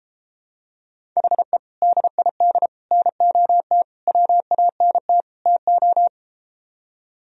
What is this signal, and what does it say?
Morse code, heard as a keyed tone, about 33 wpm, 700 Hz. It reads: HE DID NOT WANT TO